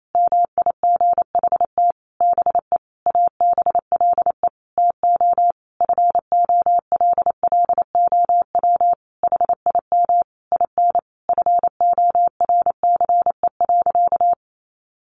Morse code, heard as a keyed tone, about 28 wpm, 700 Hz. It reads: MIGHT BE ABLE TO FOLLOW HIM IN FORCE.
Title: Sentences from Top 700 Words 28wpm